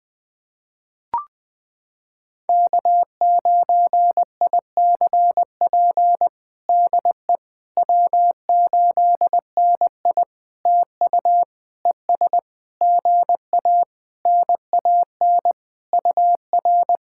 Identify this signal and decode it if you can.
Morse code, heard as a keyed tone, about 20 wpm, 700 Hz. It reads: K9ICP DE W8NI TU ES GA NAN UR
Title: POTA QSO Practice - 20wpm